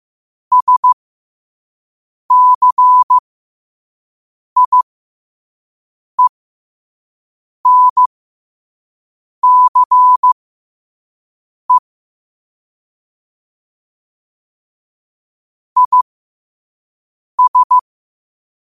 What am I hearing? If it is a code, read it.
Morse code, heard as a keyed tone, about 15 wpm, spaced as for 5 wpm, 1000 Hz. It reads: SCIENCE IS
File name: qotdfiles / 5wpm